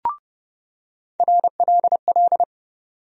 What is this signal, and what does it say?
Morse code, heard as a keyed tone, about 30 wpm, 700 Hz. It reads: RLL
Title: USA Call Sign Postfixes - 3 Letters 30wpm